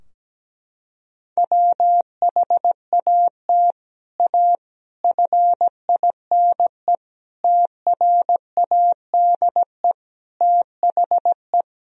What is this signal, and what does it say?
Morse code, heard as a keyed tone, about 17 wpm, 700 Hz. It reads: WHAT A FINE TRADE THE
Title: Sentences from Top 1000 Words - Review Part 4 - 17wpm